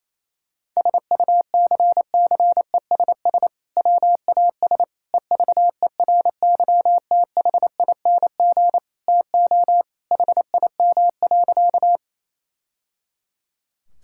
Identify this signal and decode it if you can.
Morse code, heard as a keyed tone, about 28 wpm, 700 Hz. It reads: SUCCESS WAS EVERYTHING TO HIM.
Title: Sentences from Top 1000 Words - Review Part 6 - 28wpm